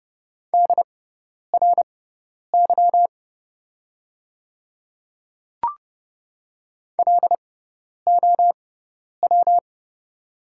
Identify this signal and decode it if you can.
Morse code, heard as a keyed tone, about 30 wpm, spaced as for 10 wpm, 700 Hz. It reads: DRY LOW
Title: 6x Character Spacing - Top 500 Words - 30wpm